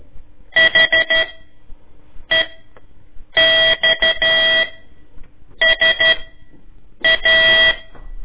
Morse code, heard as a keyed tone, about 12 wpm, spaced as for 6 wpm, 1950 Hz. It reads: HEXSA
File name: ik1weg2 / CORSI CW / lez4-1